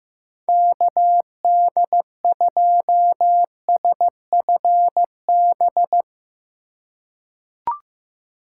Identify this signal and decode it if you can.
Morse code, heard as a keyed tone, about 16 wpm, 700 Hz. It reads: KD2SFB
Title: USA Call Signs - 6 Characters - 2x Character Spacing - 15wpm